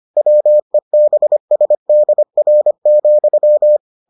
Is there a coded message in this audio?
Yes: Morse code from 0.2 to 3.8 s, heard as a keyed tone, about 25 wpm, 600 Hz. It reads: WEBSDR,